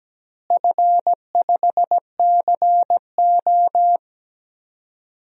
Morse code, heard as a keyed tone, about 17 wpm, 700 Hz. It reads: F5CO
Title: DXCC-100 Call Signs - 4 Characters - Repeated 2x - 17wpm